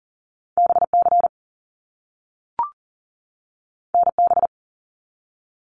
Morse code, heard as a keyed tone, about 40 wpm, 700 Hz. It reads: BC NB
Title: Canadian Provinces and Territories 40wpm